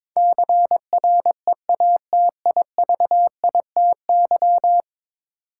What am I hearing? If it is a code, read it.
Morse code, heard as a keyed tone, about 22 wpm, 700 Hz. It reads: CREATIVITY
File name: practice / 22wpm / 10 Letter Words - First 3 Letters Given - 22wpm